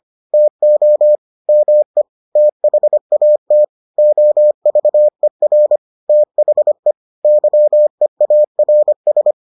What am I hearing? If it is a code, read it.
Morse code, heard as a keyed tone, about 25 wpm, 600 Hz. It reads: TO ME THAT OVER THE YEARS